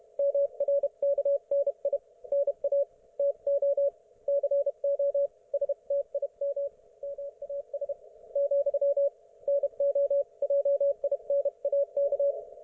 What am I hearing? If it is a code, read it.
Morse code, heard as a keyed tone, about 32 wpm, spaced as for 27 wpm, 560 Hz. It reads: MRKNI NA TO CO S TIM MAS, NO JINAK